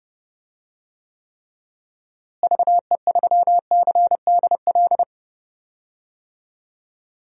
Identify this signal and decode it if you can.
Morse code, heard as a keyed tone, about 30 wpm, 700 Hz. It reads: VE3CDL
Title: DXCC-100 Call Signs - 6 Characters - Repeated 2x - 30wpm